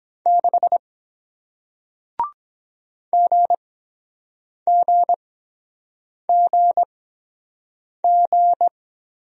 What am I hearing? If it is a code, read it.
Morse code, heard as a keyed tone, about 23 wpm, spaced as for 7 wpm, 700 Hz. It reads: 6 GGGG